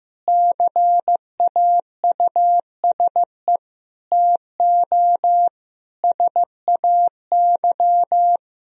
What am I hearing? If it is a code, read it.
Morse code, heard as a keyed tone, about 15 wpm, 700 Hz. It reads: CAUSE TO SAY
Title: Sentences from Top 1000 Words - Review Part 1 - 15wpm